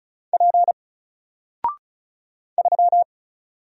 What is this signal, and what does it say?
Morse code, heard as a keyed tone, about 36 wpm, 700 Hz. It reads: P 3
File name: Single-Letter-Number-35wpm